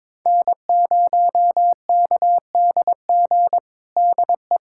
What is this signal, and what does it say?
Morse code, heard as a keyed tone, about 22 wpm, 700 Hz. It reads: N0KDG DE
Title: USA Callsign DE Callsign 22wpm